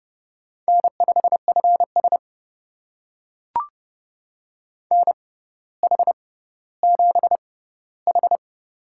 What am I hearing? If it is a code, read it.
Morse code, heard as a keyed tone, about 32 wpm, spaced as for 16 wpm, 700 Hz. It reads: N5FS NH7H
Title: USA Call Signs - 6x Character Spacing - 30wpm